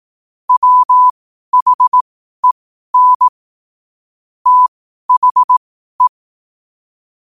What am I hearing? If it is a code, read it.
Morse code, heard as a keyed tone, about 18 wpm, spaced as for 12 wpm, 1000 Hz. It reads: WHEN THE
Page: New quotes every day in morse code at 12 Words per minute.